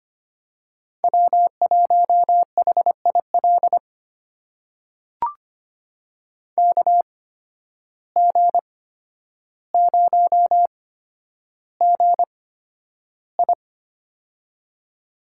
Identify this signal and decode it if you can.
Morse code, heard as a keyed tone, about 25 wpm, spaced as for 11 wpm, 700 Hz. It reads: W1HIL KG0GI
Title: USA Call Signs - 5 Characters - 8x Character Spacing - 25wpm